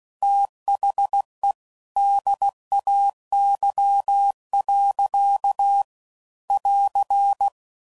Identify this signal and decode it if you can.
Morse code, heard as a keyed tone, about 16 wpm, 800 Hz. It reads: THE DAY. <AR>
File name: CW Code Groups @15WPM / Weather Sets / Weather 5 12